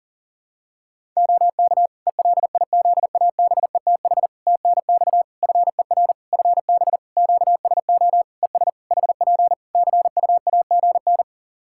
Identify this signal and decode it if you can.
Morse code, heard as a keyed tone, about 40 wpm, 700 Hz. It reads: OK ELIZABETH TNX FER FB QSO ES HP CUAGN
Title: Practice QSO - Exchange 4 - Ending - 40wpm